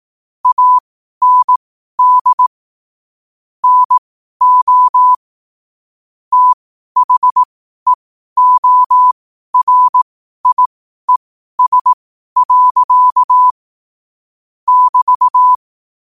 Morse code, heard as a keyed tone, about 18 wpm, spaced as for 12 wpm, 1000 Hz. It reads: AND NO THEORIES. <BT>
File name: qotdfiles / 12wpm